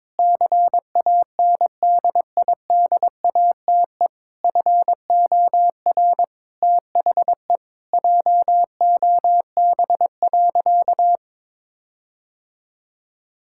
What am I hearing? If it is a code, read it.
Morse code, heard as a keyed tone, about 22 wpm, 700 Hz. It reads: CANDIDATE FOR THE JOB.
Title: Sentences from Top 1600 Words 22wpm